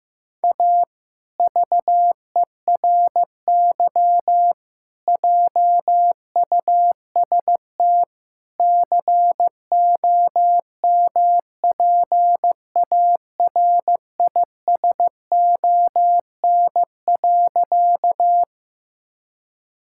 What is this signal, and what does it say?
Morse code, heard as a keyed tone, about 15 wpm, 700 Hz. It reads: A VERY JUST COMPARISON.
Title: Sentences from Top 1600 Words 15wpm